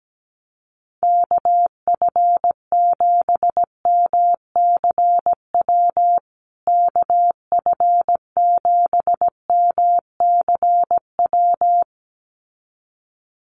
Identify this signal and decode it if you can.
Morse code, heard as a keyed tone, about 17 wpm, 700 Hz. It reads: KF7MCW KF7MCW
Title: USA Call Signs Repeated 2x 17wpm